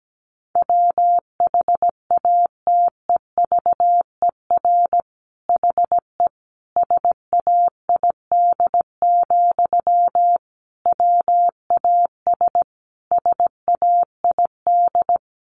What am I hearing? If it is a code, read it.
Morse code, heard as a keyed tone, about 17 wpm, 700 Hz. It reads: WHATEVER HE SAID, WAS SAID
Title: Sentences from Top 1100 Words 17wpm